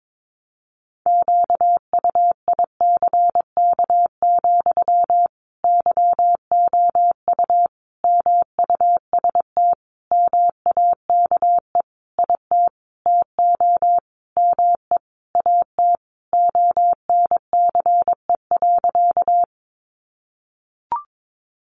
Morse code, heard as a keyed tone, about 22 wpm, 700 Hz. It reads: QUICK, YOU MUST MAKE IT TO ME AT ONCE.
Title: Sentences from Top 500 Words 22wpm